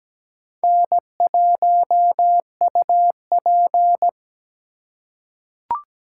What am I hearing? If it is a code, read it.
Morse code, heard as a keyed tone, about 17 wpm, 700 Hz. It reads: N1UP
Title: USA Call Signs - 4 Characters - 4x Character Spacing - 17wpm